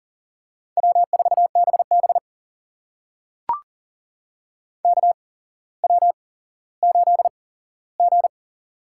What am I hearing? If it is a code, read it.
Morse code, heard as a keyed tone, about 40 wpm, spaced as for 17 wpm, 700 Hz. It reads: W4BB KW8G
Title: USA Call Signs - 4 Characters - 8x Character Spacing - 40wpm